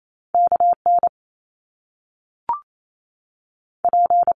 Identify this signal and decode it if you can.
Morse code, heard as a keyed tone, about 28 wpm, spaced as for 3 wpm, 700 Hz. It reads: K NP
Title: Top Two Letter Combinations in Words 28wpm